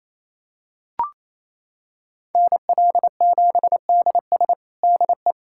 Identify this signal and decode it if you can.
Morse code, heard as a keyed tone, about 28 wpm, 700 Hz. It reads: NL7DS DE
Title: USA Callsign DE Callsign - 5 Characters - 28wpm